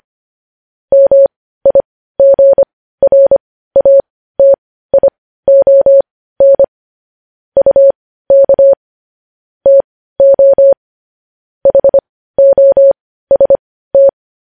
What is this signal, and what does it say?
Morse code, heard as a keyed tone, about 25 wpm, spaced as for 15 wpm, 550 Hz. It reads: MIGRATION UK TO HOST
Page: news-headlines-in-morse-code-at-15-wpm